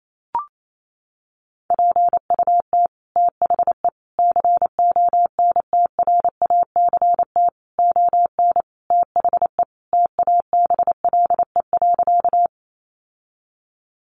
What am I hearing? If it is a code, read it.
Morse code, heard as a keyed tone, about 28 wpm, 700 Hz. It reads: PUT THE CONTRACT ON THE TABLE.
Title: Sentences from Top 1400 Words 28wpm